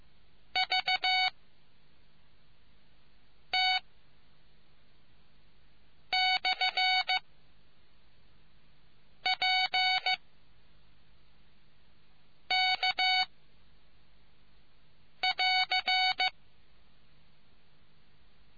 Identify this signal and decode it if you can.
Morse code, heard as a keyed tone, about 15 wpm, 750 Hz. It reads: V T / P K <AR>